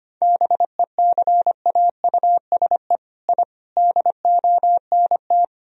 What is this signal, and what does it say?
Morse code, heard as a keyed tone, about 25 wpm, 700 Hz. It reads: BECAUSE I DONT
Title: Sentences from Top 1000 Words - Review Part 3 - 25wpm